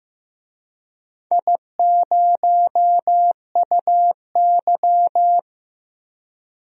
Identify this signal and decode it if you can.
Morse code, heard as a keyed tone, about 15 wpm, 700 Hz. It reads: I0UY
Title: DXCC-100 Call Signs - 4 Characters - 15wpm